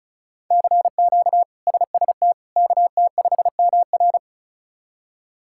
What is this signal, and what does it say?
Morse code, heard as a keyed tone, about 34 wpm, 700 Hz. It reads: CQ SST KT5MR
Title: K1USN SST QSO Practice - 35wpm